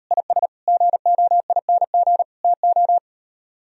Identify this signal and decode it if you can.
Morse code, heard as a keyed tone, about 38 wpm, 700 Hz. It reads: IS GOING TO